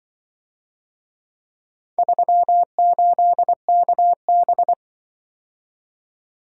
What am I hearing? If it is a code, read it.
Morse code, heard as a keyed tone, about 24 wpm, 700 Hz. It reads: 38KB
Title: Speed-Racing - Four Letter-Number Combinations - 20wpm